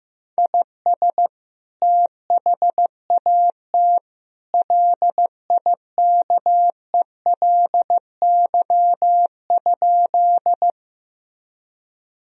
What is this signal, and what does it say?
Morse code, heard as a keyed tone, about 15 wpm, 700 Hz. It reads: IS THAT LIKELY?
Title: Sentences from Top 1000 Words - Review Part 4 - 15wpm